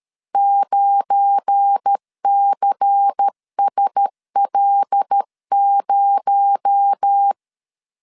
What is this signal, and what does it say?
Morse code, heard as a keyed tone, about 13 wpm, 800 Hz. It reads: 9CSL0
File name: LEZIONI / 7 GF9Z / 7 GF9 12wpm 2319